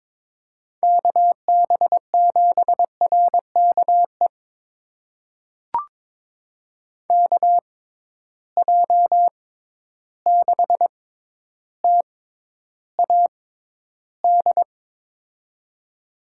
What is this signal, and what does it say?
Morse code, heard as a keyed tone, about 22 wpm, spaced as for 11 wpm, 700 Hz. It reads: KB7RKE KJ6TAD
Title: USA Call Signs - 6 Characters - 6x Character Spacing - 22wpm